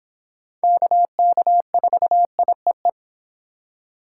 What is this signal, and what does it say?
Morse code, heard as a keyed tone, about 26 wpm, 700 Hz. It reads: KK4IEE